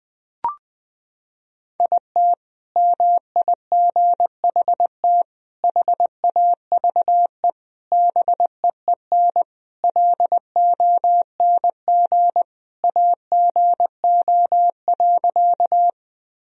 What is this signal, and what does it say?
Morse code, heard as a keyed tone, about 20 wpm, 700 Hz. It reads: IT MIGHT HAVE BEEN LONG AGO.
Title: Sentences from Top 1000 Words - Review Part 3 - 20wpm